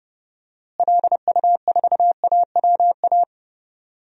Morse code, heard as a keyed tone, about 30 wpm, 700 Hz. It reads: LU4AWA
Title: DXCC-100 Call Signs - 6 Characters - Repeated 3x - 30wpm